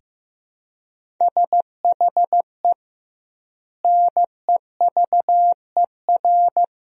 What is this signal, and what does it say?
Morse code, heard as a keyed tone, about 15 wpm, 700 Hz. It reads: SHE NEVER
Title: 2x Word-Spacing - Sentences from Top 300 Words - 15wpm